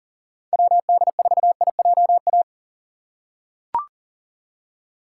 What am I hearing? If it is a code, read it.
Morse code, heard as a keyed tone, about 40 wpm, 700 Hz. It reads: WD4IJA
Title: USA Call Signs - 6 Characters - 2x Character Spacing - 40wpm